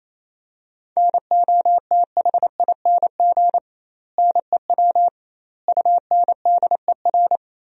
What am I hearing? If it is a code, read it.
Morse code, heard as a keyed tone, about 28 wpm, 700 Hz. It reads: NOTHING NEW UNDER
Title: 2x Word-Spacing - Sentences from Top 500 Words - 28wpm